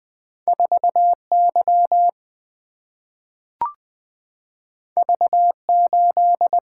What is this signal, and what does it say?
Morse code, heard as a keyed tone, about 20 wpm, 700 Hz. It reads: 4 Y V 8
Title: Single-Letter Single-Number 20wpm